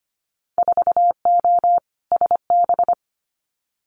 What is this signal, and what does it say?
Morse code, heard as a keyed tone, about 25 wpm, 700 Hz. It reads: 4O SB
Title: Winter Field Day Exchanges 25wpm